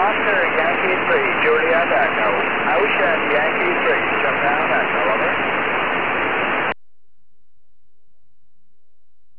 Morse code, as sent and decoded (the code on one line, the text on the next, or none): none